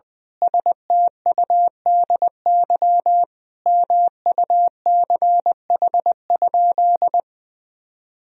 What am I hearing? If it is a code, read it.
Morse code, heard as a keyed tone, about 20 wpm, 700 Hz. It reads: STUDY MUCH?
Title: Sentences from Top 1000 Words - Review Part 1 - 20wpm